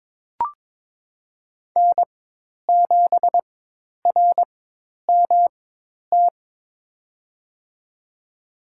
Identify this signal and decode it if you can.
Morse code, heard as a keyed tone, about 22 wpm, spaced as for 10 wpm, 700 Hz. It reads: N7RMT
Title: USA Call Signs - 5 Characters - 4x Character Spacing - 22wpm